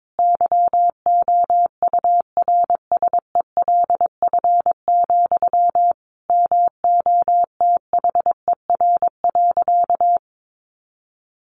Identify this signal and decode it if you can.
Morse code, heard as a keyed tone, about 22 wpm, 700 Hz. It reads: YOURSELF, MOTHER.
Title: Sentences from Top 1300 Words 22wpm